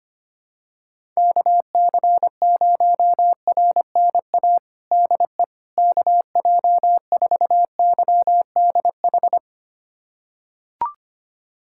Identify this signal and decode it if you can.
Morse code, heard as a keyed tone, about 25 wpm, 700 Hz. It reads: KC0RNA DE KJ4YDH
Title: USA Callsign DE Callsign 25wpm